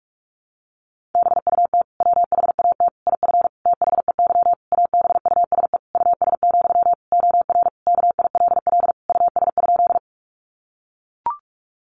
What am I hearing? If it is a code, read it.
Morse code, heard as a keyed tone, about 44 wpm, 700 Hz. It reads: BUT WHAT IF THEY ABUSE US, OR KILL US?
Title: Sentences from Top 1400 Words 45wpm